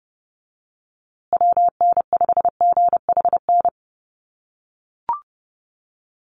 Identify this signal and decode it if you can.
Morse code, heard as a keyed tone, about 30 wpm, 700 Hz. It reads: WN5GHN
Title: USA Call Signs Repeated 2x 30wpm